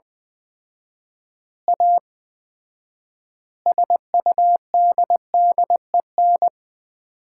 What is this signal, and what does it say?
Morse code, heard as a keyed tone, about 21 wpm, 700 Hz. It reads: A SUDDEN
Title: 4x Word-Spacing - Sets of 5 Words - 20wpm